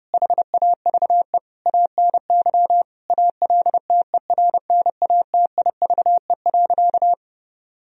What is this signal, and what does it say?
Morse code, heard as a keyed tone, about 30 wpm, 700 Hz. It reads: HAVE ANY ALTERNATIVE.
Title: Sentences from Top 1000 Words - Review Part 4 - 30wpm